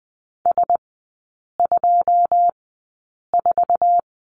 Morse code, heard as a keyed tone, about 20 wpm, spaced as for 8 wpm, 700 Hz. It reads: S24
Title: 138 Lesson 23e - Letter f - Three Characters 20wpm